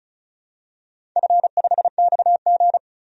Morse code, heard as a keyed tone, about 36 wpm, 700 Hz. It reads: F5XG
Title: DXCC-100 Call Signs - 4 Characters - 35wpm